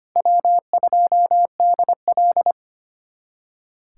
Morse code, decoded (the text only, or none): W2DL